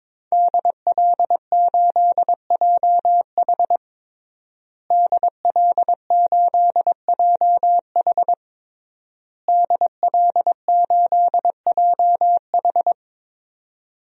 Morse code, heard as a keyed tone, about 22 wpm, 700 Hz. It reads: DL8JH DL8JH DL8JH
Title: DXCC-100 Call Signs - 5 Characters - Repeated 3x - 22wpm